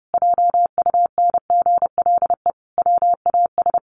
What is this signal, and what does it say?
Morse code, heard as a keyed tone, about 30 wpm, 700 Hz. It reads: JUNGLE WAS
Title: A Sound Of Thunder 30wpm